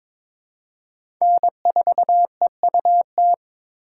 Morse code, heard as a keyed tone, about 22 wpm, 700 Hz. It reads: N4EUT
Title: USA Call Signs - 4x Character Spacing - 22wpm